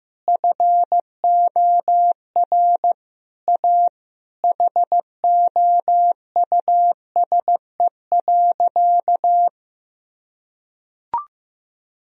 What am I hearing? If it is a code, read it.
Morse code, heard as a keyed tone, about 15 wpm, 700 Hz. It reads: FOR A HOUSE.
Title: Sentences from Top 1000 Words - Review Part 3 - 15wpm